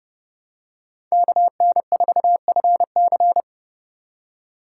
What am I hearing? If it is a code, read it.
Morse code, heard as a keyed tone, about 30 wpm, 700 Hz. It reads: KN4FC
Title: USA Call Signs - 5 Characters - 6x Character Spacing - 30wpm